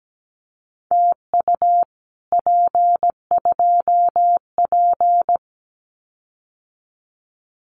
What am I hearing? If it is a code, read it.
Morse code, heard as a keyed tone, about 17 wpm, 700 Hz. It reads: TU P2P
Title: Common QSO Phrases Encore 17wpm